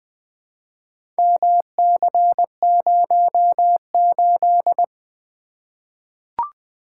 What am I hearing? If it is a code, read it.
Morse code, heard as a keyed tone, about 20 wpm, 700 Hz. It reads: MC08
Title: Four Letter-Number Combinations - 20wpm